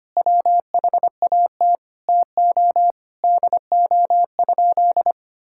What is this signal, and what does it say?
Morse code, heard as a keyed tone, about 25 wpm, 700 Hz. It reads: WHAT TO DO?
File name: practice / 25wpm / Sentences with 2-Word Phrases - 25wpm